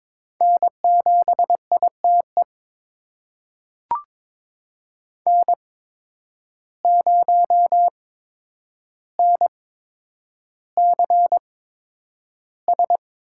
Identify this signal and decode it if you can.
Morse code, heard as a keyed tone, about 22 wpm, spaced as for 9 wpm, 700 Hz. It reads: N7ITE N0NCS
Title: USA Call Signs - 5 Characters - 8x Character Spacing - 22wpm